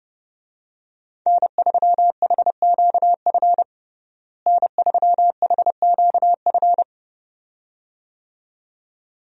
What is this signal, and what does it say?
Morse code, heard as a keyed tone, about 30 wpm, 700 Hz. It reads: N3HQF N3HQF
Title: USA Call Signs - 5 Characters - Repeated 2x - 30wpm